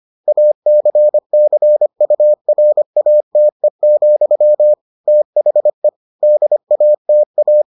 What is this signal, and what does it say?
Morse code, heard as a keyed tone, about 25 wpm, 600 Hz. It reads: ACCURATE, THE DATA